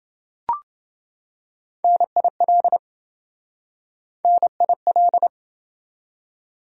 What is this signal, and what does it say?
Morse code, heard as a keyed tone, about 26 wpm, 700 Hz. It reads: NIL NIL